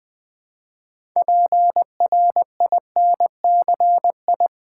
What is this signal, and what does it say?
Morse code, heard as a keyed tone, about 20 wpm, 700 Hz. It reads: PRINCI
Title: Build A Word - 3 Letters At A Time - 9-Letter Words - 20wpm